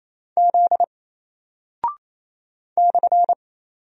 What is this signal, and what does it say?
Morse code, heard as a keyed tone, about 28 wpm, 700 Hz. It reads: Z /
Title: Single Character (Letter:Number:Punctuation:Prosign) Alt 28wpm